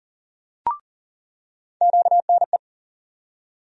Morse code, heard as a keyed tone, about 40 wpm, 700 Hz. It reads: QNE
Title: USA Call Sign Postfixes - 3 Letters 40wpm